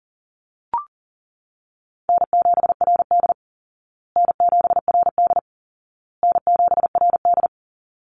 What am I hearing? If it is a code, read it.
Morse code, heard as a keyed tone, about 37 wpm, 700 Hz. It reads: N7RD N7RD N7RD
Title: Easy Call Signs Repeated 3x - 40wpm